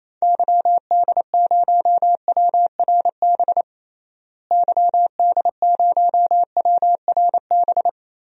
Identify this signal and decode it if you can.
Morse code, heard as a keyed tone, about 28 wpm, 700 Hz. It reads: YD0WRB YD0WRB
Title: DXCC-100 Call Signs - 6 Characters - Repeated 3x - 28wpm